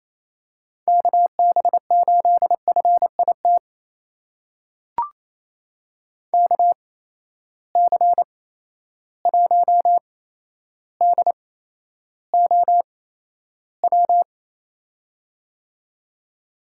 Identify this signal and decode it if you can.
Morse code, heard as a keyed tone, about 28 wpm, spaced as for 12 wpm, 700 Hz. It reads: KB8FIT KC1DOW
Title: USA Call Signs - 6 Characters - 8x Character Spacing - 28wpm